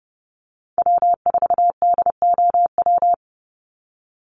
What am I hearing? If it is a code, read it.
Morse code, heard as a keyed tone, about 30 wpm, 700 Hz. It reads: W4DOW
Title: NA SOTA Chaser Callsigns 30wpm